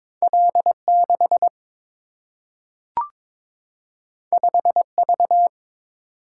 Morse code, heard as a keyed tone, about 22 wpm, 700 Hz. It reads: L6 5V
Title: Single-Letter Single-Number 22wpm